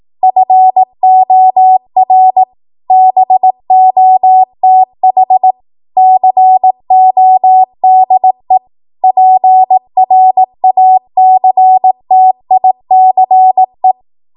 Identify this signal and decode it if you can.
Morse code, heard as a keyed tone, about 18 wpm, 750 Hz. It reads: FOR BOTH CODE PRACTICE
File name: Archive / 18 WPM / 18WPM